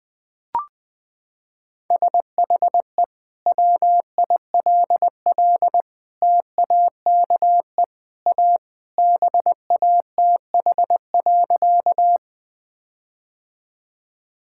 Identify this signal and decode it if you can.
Morse code, heard as a keyed tone, about 21 wpm, 700 Hz. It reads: SHE WILL TAKE A BATH.
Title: Sentences from Top 1600 Words 20wpm